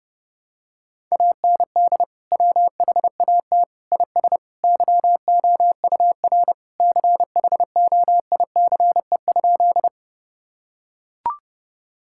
Morse code, heard as a keyed tone, about 28 wpm, 700 Hz. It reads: AND WHAT IS YOUR CHOICE?
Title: Sentences from Top 1000 Words - Review Part 6 - 30wpm